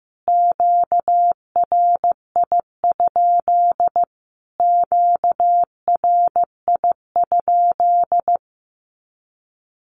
Morse code, heard as a keyed tone, about 15 wpm, 700 Hz. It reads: QRI? QRI?
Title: Q-Codes Repeated 3x 15wpm